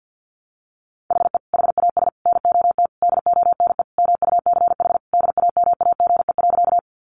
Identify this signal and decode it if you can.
Morse code, heard as a keyed tone, about 48 wpm, 700 Hz. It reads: HE HAS NOT DONE MUCH DAMAGE.
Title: Sentences from Top 1100 Words 50wpm